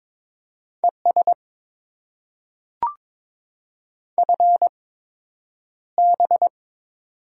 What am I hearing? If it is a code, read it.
Morse code, heard as a keyed tone, about 22 wpm, spaced as for 9 wpm, 700 Hz. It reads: ES FB
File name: Top 20 QSO Elements - 8x Character Spacing - 22wpm